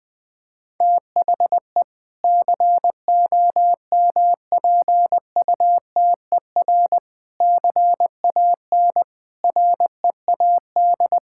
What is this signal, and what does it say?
Morse code, heard as a keyed tone, about 20 wpm, 700 Hz. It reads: THE COMPUTER CAN READ
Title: Sentences from Top 1000 Words - Review Part 6 - 20wpm